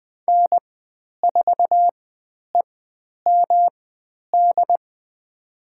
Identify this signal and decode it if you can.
Morse code, heard as a keyed tone, about 20 wpm, spaced as for 10 wpm, 700 Hz. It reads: N4EMD
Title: 109 Lesson 18g - Letter m - Call Signs 20wpm